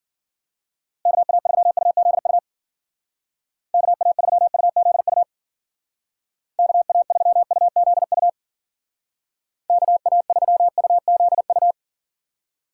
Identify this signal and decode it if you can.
Morse code, heard as a keyed tone, about 50 wpm, 700 Hz. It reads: KA3UZU KA3UZU KA3UZU KA3UZU
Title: Speed Racing - USA 2x3 Call Signs 40wpm